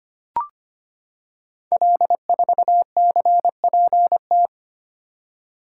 Morse code, heard as a keyed tone, about 25 wpm, 700 Hz. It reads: L4CPT
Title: Code Groups - 5 Letter:Numbers - 25wpm